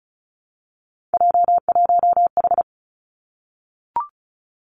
Morse code, heard as a keyed tone, about 36 wpm, 700 Hz. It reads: J1H
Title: Three Challenging Letter-Number Combinations 35wpm